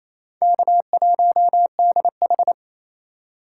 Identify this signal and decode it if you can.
Morse code, heard as a keyed tone, about 27 wpm, 700 Hz. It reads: K1DH
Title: USA Call Signs - 2x Character Spacing - 28wpm